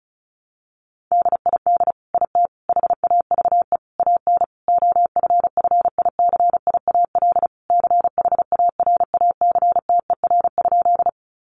Morse code, heard as a keyed tone, about 34 wpm, 700 Hz. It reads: DID IT HAVE AN OFFICIAL CHARACTER?
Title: Sentences from Top 1200 Words 35wpm